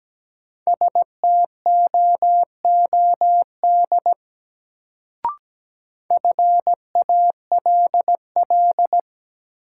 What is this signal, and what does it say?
Morse code, heard as a keyed tone, about 17 wpm, 700 Hz. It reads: STOOD FALL
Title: Sentences from Top 500 Words - New Word Drill - 17wpm